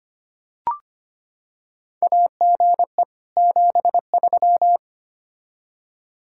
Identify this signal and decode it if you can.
Morse code, heard as a keyed tone, about 25 wpm, 700 Hz. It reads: AGE 73
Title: Practice QSO - Exchange 3 - More Info - 25wpm